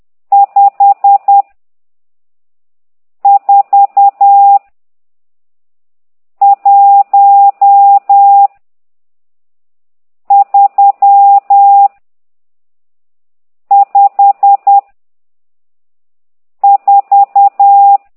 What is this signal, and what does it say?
Morse code, heard as a keyed tone, about 10 wpm, 800 Hz. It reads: 5 4 1 3 5 4
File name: ct2idw / exerc12